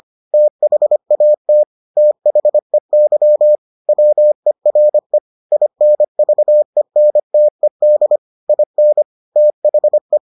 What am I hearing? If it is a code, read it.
Morse code, heard as a keyed tone, about 25 wpm, 600 Hz. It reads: THAT THEY WERE INVENTED IN THE